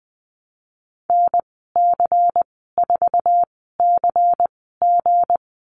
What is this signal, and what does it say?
Morse code, heard as a keyed tone, about 20 wpm, spaced as for 15 wpm, 700 Hz. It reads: NC4CG
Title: 193 Lesson 31g - Letter v - Call Signs 20wpm